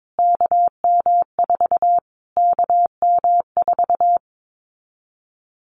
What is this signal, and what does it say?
Morse code, heard as a keyed tone, about 22 wpm, 700 Hz. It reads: KM4 KM4
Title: Callsign Prefixes for Top 25 DXCC - Repeated 3x - 22wpm